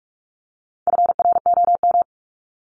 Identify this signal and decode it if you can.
Morse code, heard as a keyed tone, about 44 wpm, 700 Hz. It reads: FROM
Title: Top 300 Words Repeated 45wpm